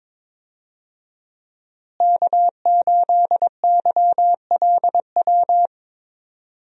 Morse code, heard as a keyed tone, about 22 wpm, 700 Hz. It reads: K8YLW
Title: USA Call Signs - 5 Characters - Repeated 3x - 22wpm